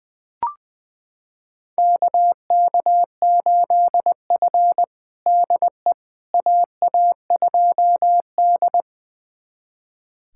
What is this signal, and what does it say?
Morse code, heard as a keyed tone, about 20 wpm, 700 Hz. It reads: KK8F DE AA2D
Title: USA Callsign DE Callsign - 4 Characters - 20wpm